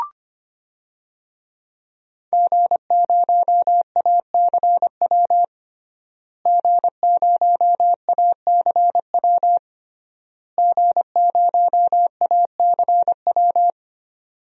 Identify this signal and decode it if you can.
Morse code, heard as a keyed tone, about 25 wpm, 700 Hz. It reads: G0ACW G0ACW G0ACW
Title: DXCC-100 Call Signs - 5 Characters - Repeated 3x - 25wpm